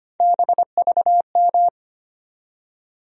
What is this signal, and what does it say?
Morse code, heard as a keyed tone, about 25 wpm, 700 Hz. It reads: BVM
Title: USA Call Sign Postfixes - 3 Letters 25wpm